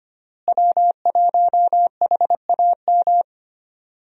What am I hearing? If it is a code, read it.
Morse code, heard as a keyed tone, about 25 wpm, 700 Hz. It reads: W1HAM
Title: USA Call Signs - 5 Characters - Repeated 2x - 25wpm